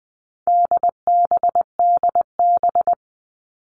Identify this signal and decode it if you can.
Morse code, heard as a keyed tone, about 20 wpm, 700 Hz. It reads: DBDB
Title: All Cut Numbers - 4-Digit 20wpm